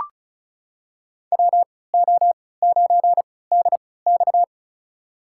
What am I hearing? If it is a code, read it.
Morse code, heard as a keyed tone, about 36 wpm, spaced as for 20 wpm, 700 Hz. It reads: WO9DX
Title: USA Call Signs - 5 Characters - 3x Character Spacing - 35wpm